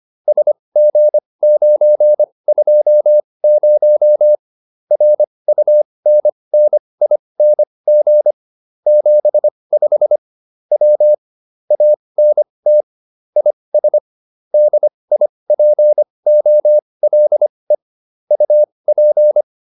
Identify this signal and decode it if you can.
Morse code, heard as a keyed tone, about 25 wpm, spaced as for 20 wpm, 600 Hz. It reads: SG920 RUNNING 75 W ANT IS DIPOLE UP